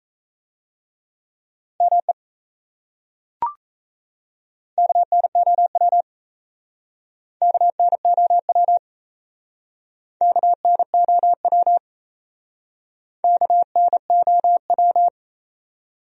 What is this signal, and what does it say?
Morse code, heard as a keyed tone, about 36 wpm, 700 Hz. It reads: ME KNOW KNOW KNOW KNOW